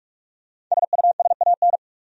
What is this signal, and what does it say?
Morse code, heard as a keyed tone, about 45 wpm, 700 Hz. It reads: SUSAN
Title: USA Common Names - Max 5 Letters - 45wpm